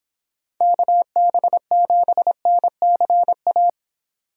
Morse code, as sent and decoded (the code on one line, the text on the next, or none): -.- -... --... -. -.-. .-
KB7NCA